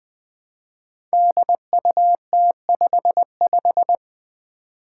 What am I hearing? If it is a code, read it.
Morse code, heard as a keyed tone, about 20 wpm, 700 Hz. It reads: DUT55
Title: Five Challenging Letter-Number Combinations 20wpm